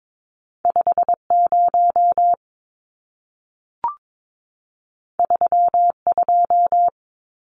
Morse code, heard as a keyed tone, about 22 wpm, 700 Hz. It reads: 50 32
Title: Numbers - 2-Digit 22wpm